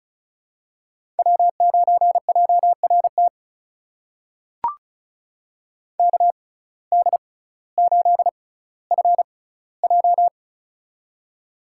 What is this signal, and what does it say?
Morse code, heard as a keyed tone, about 34 wpm, spaced as for 18 wpm, 700 Hz. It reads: W9JRT KD8FJ
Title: USA Call Signs - 5 Characters - 6x Character Spacing - 35wpm